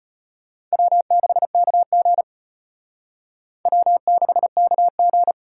Morse code, heard as a keyed tone, about 29 wpm, 700 Hz. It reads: W6KG W6KG
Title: Speed Racing - USA 1x2 Call Signs 25wpm